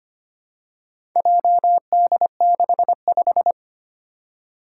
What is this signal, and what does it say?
Morse code, heard as a keyed tone, about 25 wpm, 700 Hz. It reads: JD65
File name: practice / 25wpm / Four Letter-Number Combinations - 25wpm